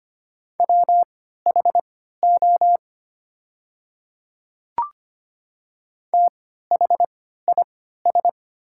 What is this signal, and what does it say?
Morse code, heard as a keyed tone, about 25 wpm, spaced as for 14 wpm, 700 Hz. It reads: WHO THIS
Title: 3x Character Spacing - Top 200 Words - 25wpm